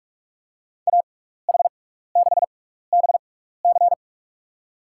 Morse code, heard as a keyed tone, about 48 wpm, spaced as for 16 wpm, 700 Hz. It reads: AH6BC
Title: USA Call Signs - 5 Characters - 6x Character Spacing - 45wpm